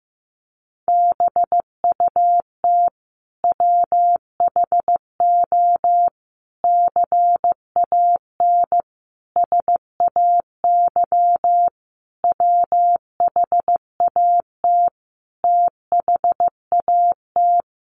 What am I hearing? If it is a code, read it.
Morse code, heard as a keyed tone, about 15 wpm, 700 Hz. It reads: BUT WHO CAN SAY WHAT THAT
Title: Sentences from Top 1500 Words 15wpm